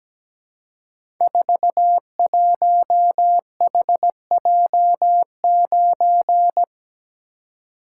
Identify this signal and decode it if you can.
Morse code, heard as a keyed tone, about 17 wpm, 700 Hz. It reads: 41HJ9
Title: Code Groups - 5 Letter:Numbers - 17wpm